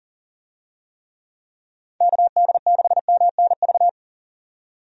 Morse code, heard as a keyed tone, about 39 wpm, 700 Hz. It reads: KD6MNV
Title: USA Call Signs - 6 Characters - 40wpm